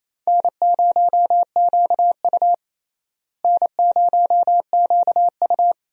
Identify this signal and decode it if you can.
Morse code, heard as a keyed tone, about 28 wpm, 700 Hz. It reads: N0QU N0QU
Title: USA Call Signs - 4 Characters - Repeated 2x - 28wpm